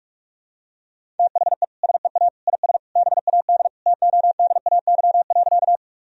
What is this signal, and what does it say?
Morse code, heard as a keyed tone, about 46 wpm, 700 Hz. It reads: THE SEA IS BAD TODAY.
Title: Sentences from Top 1000 Words - Review Part 1 - 45wpm